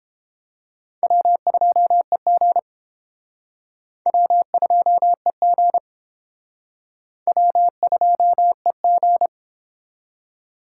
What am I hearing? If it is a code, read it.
Morse code, heard as a keyed tone, about 28 wpm, 700 Hz. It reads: W2EG W2EG W2EG